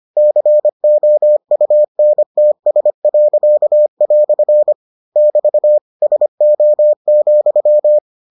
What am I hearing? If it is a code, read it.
Morse code, heard as a keyed tone, about 25 wpm, 600 Hz. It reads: COUNTS." <BT> SO,